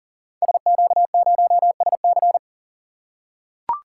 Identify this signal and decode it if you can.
Morse code, heard as a keyed tone, about 39 wpm, 700 Hz. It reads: SQ0SC
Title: Code Groups - 5 Letter:Numbers - 40wpm